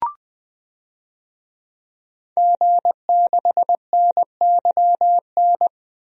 Morse code, heard as a keyed tone, about 20 wpm, 700 Hz. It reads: G6NYN
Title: DXCC-100 Call Signs - 5 Characters - 20wpm